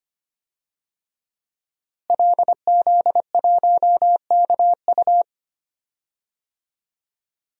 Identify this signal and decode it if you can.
Morse code, heard as a keyed tone, about 25 wpm, 700 Hz. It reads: LZ1KU
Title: DXCC-100 Call Signs - 5 Characters - Repeated 3x - 25wpm